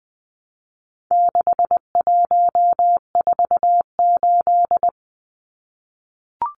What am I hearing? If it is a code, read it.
Morse code, heard as a keyed tone, about 20 wpm, 700 Hz. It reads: 6148
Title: Numbers - 4-Digit 20wpm